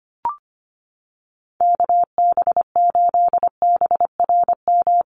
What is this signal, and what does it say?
Morse code, heard as a keyed tone, about 25 wpm, 700 Hz. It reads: KB8BRM
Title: USA Call Signs Repeated 2x 25wpm